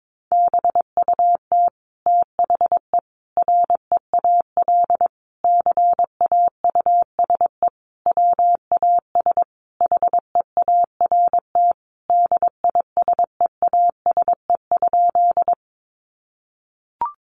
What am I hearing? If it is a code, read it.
Morse code, heard as a keyed tone, about 22 wpm, 700 Hz. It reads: BUT THE REAL CAUSE WAS HEART DISEASE?
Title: Sentences from Top 1200 Words 22wpm